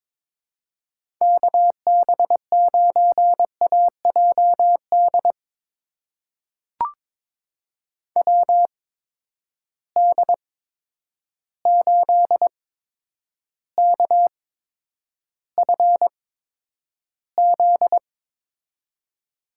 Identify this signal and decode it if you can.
Morse code, heard as a keyed tone, about 22 wpm, spaced as for 9 wpm, 700 Hz. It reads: KB9AJD WD8KFZ
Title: USA Call Signs - 6 Characters - 8x Character Spacing - 22wpm